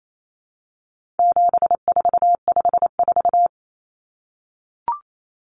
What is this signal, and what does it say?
Morse code, heard as a keyed tone, about 28 wpm, 700 Hz. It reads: 7454
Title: Numbers - 4-Digit 28wpm